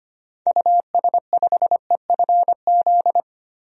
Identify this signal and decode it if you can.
Morse code, heard as a keyed tone, about 25 wpm, 700 Hz. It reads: US5EFZ
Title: DXCC-100 Call Signs - 6 Characters - 25wpm